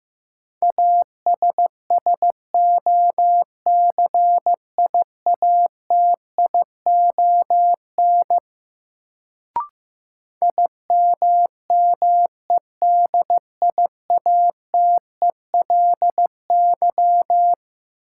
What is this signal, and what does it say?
Morse code, heard as a keyed tone, about 15 wpm, 700 Hz. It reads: ASSOCIATION IMMEDIATELY
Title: Sentences from Top 1100 Words - New Word Drill - 15wpm